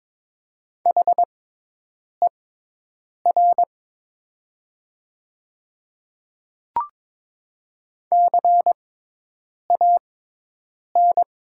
Morse code, heard as a keyed tone, about 22 wpm, spaced as for 8 wpm, 700 Hz. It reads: HER CAN
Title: 6x Character Spacing - Top 100 Words - 22wpm